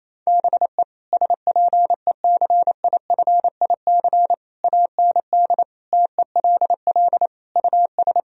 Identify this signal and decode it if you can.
Morse code, heard as a keyed tone, about 28 wpm, 700 Hz. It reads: BE SPECIFIC AND TELL US
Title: Sentences from Top 1000 Words - Review Part 3 - 28wpm